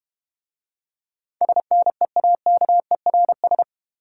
Morse code, heard as a keyed tone, about 31 wpm, 700 Hz. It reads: SNEAKERS
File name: Easy Words - Speed-Racing with Hints - 25wpm